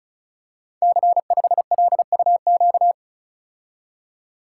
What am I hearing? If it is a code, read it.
Morse code, heard as a keyed tone, about 36 wpm, 690 Hz. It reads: C5LUQ
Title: Code Groups - 5 Letter:Numbers - 35wpm